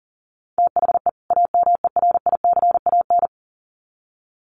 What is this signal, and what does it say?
Morse code, heard as a keyed tone, about 40 wpm, 700 Hz. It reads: THE AMERICAN
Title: Sets of 2 Words 40wpm